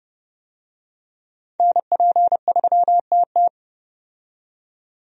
Morse code, heard as a keyed tone, about 30 wpm, 700 Hz. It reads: NP3TT
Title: USA Call Signs - 5 Characters - 30wpm